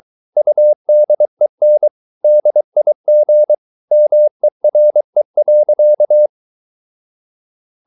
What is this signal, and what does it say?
Morse code, heard as a keyed tone, about 23 wpm, 600 Hz. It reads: UDEN DIG MERE.
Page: Ferie 23 wpm